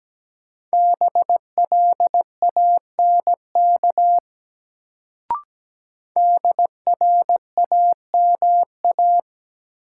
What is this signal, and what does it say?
Morse code, heard as a keyed tone, about 17 wpm, 700 Hz. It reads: BLANK DRAMA
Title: Sentences from Top 1500 Words - New Word Drill Alt - 17wpm